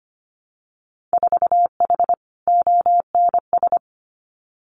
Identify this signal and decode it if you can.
Morse code, heard as a keyed tone, about 25 wpm, 700 Hz. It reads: 4H ONS
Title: Winter Field Day Exchanges Encore 25wpm